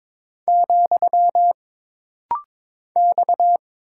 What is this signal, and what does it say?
Morse code, heard as a keyed tone, about 22 wpm, 700 Hz. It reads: , X
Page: This rapid-fire timing is geared to push your ICR (Instant Character Recognition) skills.